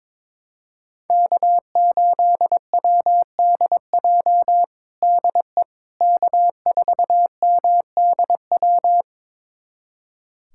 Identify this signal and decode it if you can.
Morse code, heard as a keyed tone, about 21 wpm, 700 Hz. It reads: K8WDJ DE K4MDW
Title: USA Callsign DE Callsign - 5 Characters - 22wpm